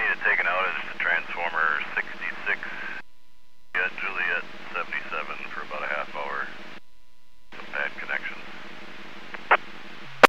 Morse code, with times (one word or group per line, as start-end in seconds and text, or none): none